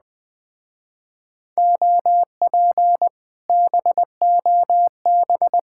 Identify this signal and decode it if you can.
Morse code, heard as a keyed tone, about 20 wpm, 700 Hz. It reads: OP BOB
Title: Speed-Racing Common QSO Phrases Encore - 17wpm